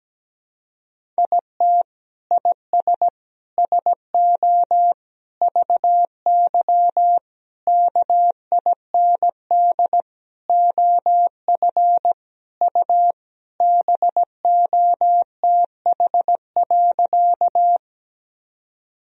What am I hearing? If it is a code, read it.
Morse code, heard as a keyed tone, about 17 wpm, 700 Hz. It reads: IT IS SO VY KIND OF U BOTH.
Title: Sentences from Top 300 Words and Abbreviations 17wpm